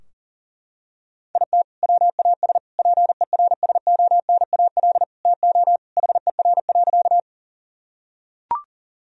Morse code, heard as a keyed tone, about 37 wpm, 700 Hz. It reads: IT WAS PERSONAL TO HER.
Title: Sentences from Top 1000 Words - Review Part 4 - 40wpm